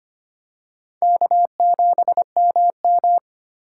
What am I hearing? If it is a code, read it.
Morse code, heard as a keyed tone, about 25 wpm, 700 Hz. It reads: K7MM
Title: USA Call Signs - 4 Characters - 6x Character Spacing - 25wpm